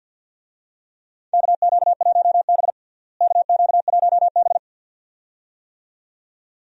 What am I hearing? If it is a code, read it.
Morse code, heard as a keyed tone, about 51 wpm, 700 Hz. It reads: KQ1B KQ1B
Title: USA Call Signs - 4 Characters - Repeated 2x - 50wpm